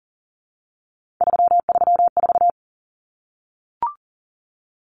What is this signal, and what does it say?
Morse code, heard as a keyed tone, about 40 wpm, 700 Hz. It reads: 334
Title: Numbers - 3-Digit 40wpm